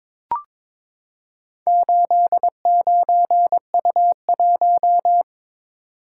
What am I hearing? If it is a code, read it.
Morse code, heard as a keyed tone, about 22 wpm, 700 Hz. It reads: 89U1
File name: practice / 22wpm / Four Letter-Number Combinations - 22wpm